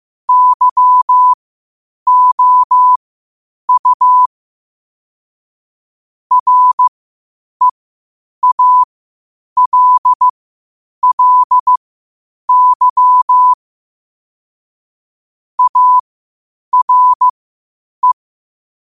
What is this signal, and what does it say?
Morse code, heard as a keyed tone, about 15 wpm, spaced as for 8 wpm, 1000 Hz. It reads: YOU REALLY ARE
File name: qotdfiles / 8wpm